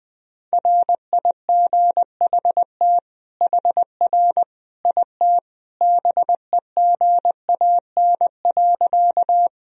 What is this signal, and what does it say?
Morse code, heard as a keyed tone, about 20 wpm, 700 Hz. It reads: RIGHT HR IT BEGAN.
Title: Sentences from Top 300 Words and Abbreviations 20wpm